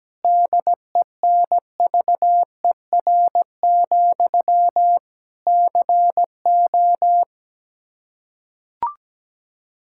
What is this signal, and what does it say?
Morse code, heard as a keyed tone, about 17 wpm, 700 Hz. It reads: DENVER, CO
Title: K1USN SST QSO Practice 17wpm